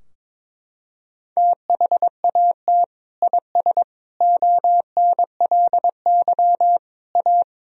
Morse code, heard as a keyed tone, about 22 wpm, 700 Hz. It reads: THAT IS ONLY A
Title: Sentences from Top 1000 Words - Review Part 4 - 22wpm